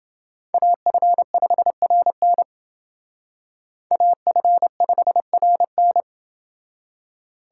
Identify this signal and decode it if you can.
Morse code, heard as a keyed tone, about 28 wpm, 700 Hz. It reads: AF5RN AF5RN